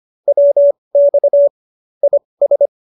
Morse code, heard as a keyed tone, about 25 wpm, spaced as for 20 wpm, 590 Hz. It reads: WX IS